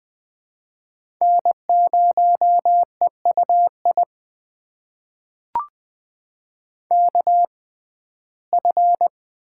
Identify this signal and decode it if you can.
Morse code, heard as a keyed tone, about 20 wpm, spaced as for 15 wpm, 700 Hz. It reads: N0EUI KF
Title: USA Call Signs - 5 Characters - 6x Character Spacing - 20wpm